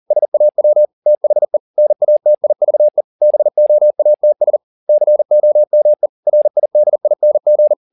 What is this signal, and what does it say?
Morse code, heard as a keyed tone, about 40 wpm, 600 Hz. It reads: SAW THE NATIVE BOATS COME RIDING